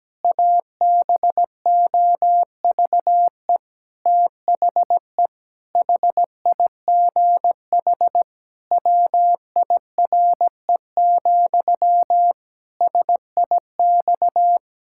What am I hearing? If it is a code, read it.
Morse code, heard as a keyed tone, about 17 wpm, 700 Hz. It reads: ABOVE THE HIGH WIRE, SIX
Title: Pangram Challenge v2 - 17wpm